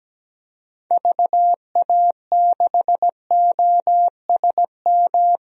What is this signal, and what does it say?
Morse code, heard as a keyed tone, about 17 wpm, 700 Hz. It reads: VA6OSM
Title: DXCC-100 Call Signs - 6 Characters - Repeated 2x - 17wpm